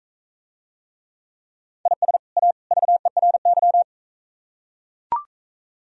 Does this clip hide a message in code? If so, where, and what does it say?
Yes: Morse code from 1.9 to 3.8 s, heard as a keyed tone, about 45 wpm, 700 Hz. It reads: IS A VERY